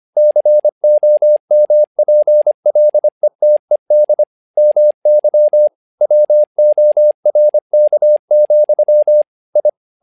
Morse code, heard as a keyed tone, about 25 wpm, 600 Hz. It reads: COMPLETED MY WORK, I
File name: learning center / tower trouble / tower trouble 25wpm